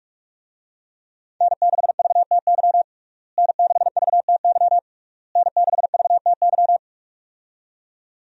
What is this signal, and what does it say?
Morse code, heard as a keyed tone, about 42 wpm, 700 Hz. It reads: N6VTY N6VTY N6VTY
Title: USA Call Signs - 5 Characters - Repeated 3x - 45wpm